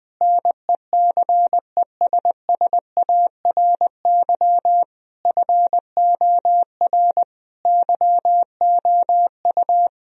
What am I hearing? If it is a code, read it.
Morse code, heard as a keyed tone, about 20 wpm, 700 Hz. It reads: NECESSARY FOR YOU
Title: Sentences from Top 1000 Words - Review Part 1 - 20wpm